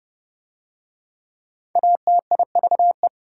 Morse code, heard as a keyed tone, about 30 wpm, 700 Hz. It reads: ATIVE